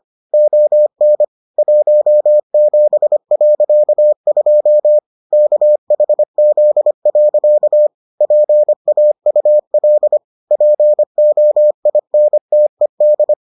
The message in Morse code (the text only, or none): ON 17.2 KHZ. PAUL POINTED